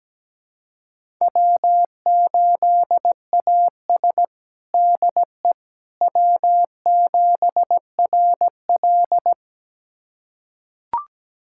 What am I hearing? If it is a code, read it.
Morse code, heard as a keyed tone, about 17 wpm, 700 Hz. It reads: W8AS DE W7RL
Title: USA Callsign DE Callsign - 4 Characters - 17wpm